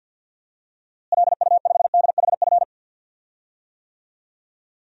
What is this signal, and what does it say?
Morse code, heard as a keyed tone, about 52 wpm, 700 Hz. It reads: LU5DHF
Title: DXCC-100 Call Signs - 6 Characters - Repeated 3x - 50wpm